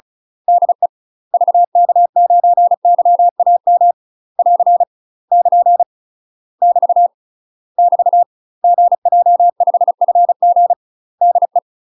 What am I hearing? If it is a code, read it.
Morse code, heard as a keyed tone, about 35 wpm, 700 Hz. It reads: DE VK9YAM <AR> <KN> <BT> <BT> GJ5FG DE
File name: UK MASTER QSOs TRAINING / 35WPM / UK MASTER QSOs 0023